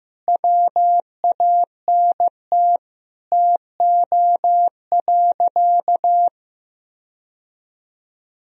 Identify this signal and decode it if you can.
Morse code, heard as a keyed tone, about 15 wpm, 700 Hz. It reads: WANT TO.
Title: Sentences from Top 1000 Words - Review Part 3 - 15wpm